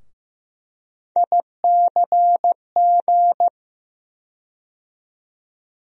Morse code, heard as a keyed tone, about 15 wpm, 700 Hz. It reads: ICG
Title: USA Call Sign Postfixes - 3 Letters 15wpm